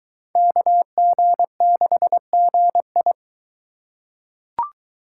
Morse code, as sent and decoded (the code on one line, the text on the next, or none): -.- --. -.... --. ..
KG6GI